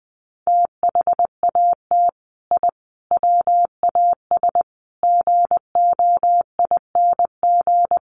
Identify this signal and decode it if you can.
Morse code, heard as a keyed tone, about 20 wpm, 700 Hz. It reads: THAT I WAS GOING
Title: Sets of 5 Words Encore 20wpm